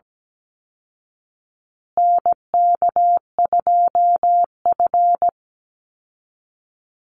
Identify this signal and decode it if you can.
Morse code, heard as a keyed tone, about 17 wpm, 700 Hz. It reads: NK2F
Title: Call Signs Repeated 3x - 17wpm